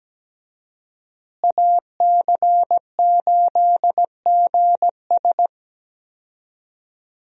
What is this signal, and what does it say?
Morse code, heard as a keyed tone, about 17 wpm, 700 Hz. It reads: AC8GS
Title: USA Call Signs - 5 Characters - Repeated 2x - 17wpm